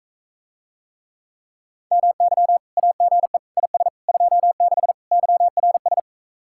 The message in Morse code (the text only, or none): MY AGE IS 26 YRS